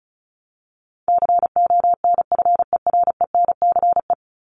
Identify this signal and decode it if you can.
Morse code, heard as a keyed tone, about 34 wpm, 700 Hz. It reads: CONFERENCE
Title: Top 1751-2000 Words 35wpm